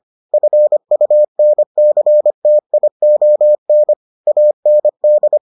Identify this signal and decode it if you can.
Morse code, heard as a keyed tone, about 25 wpm, 600 Hz. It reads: FUNCTION AND